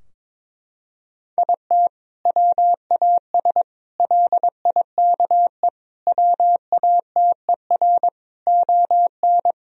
Morse code, heard as a keyed tone, about 22 wpm, 700 Hz. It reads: IT WAS LIKE WATER ON
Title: Sentences from Top 1000 Words - Review Part 4 - 22wpm